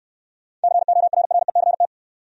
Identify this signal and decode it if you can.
Morse code, heard as a keyed tone, about 68 wpm, 700 Hz. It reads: CORRECT